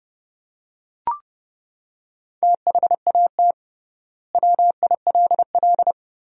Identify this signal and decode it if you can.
Morse code, heard as a keyed tone, about 31 wpm, 700 Hz. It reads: THAT WILL
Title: 3x Word-Spacing - Sentences from Top 100 Words - 30wpm